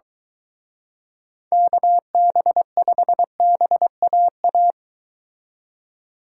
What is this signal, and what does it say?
Morse code, heard as a keyed tone, about 23 wpm, 700 Hz. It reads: KB5BAA